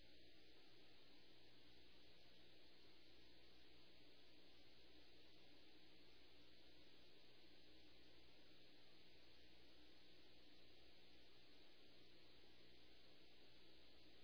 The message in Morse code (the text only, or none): none